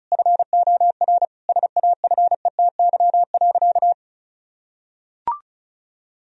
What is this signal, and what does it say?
Morse code, heard as a keyed tone, about 34 wpm, 700 Hz. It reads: FOR SAFETY.
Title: Sentences from Top 1000 Words - Review Part 4 - 35wpm